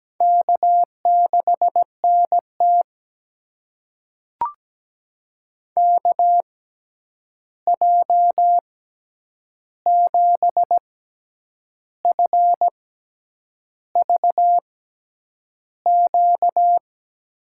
Morse code, heard as a keyed tone, about 17 wpm, spaced as for 8 wpm, 700 Hz. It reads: K6NT KJ7FVQ
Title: USA Call Signs - 6x Character Spacing - 17wpm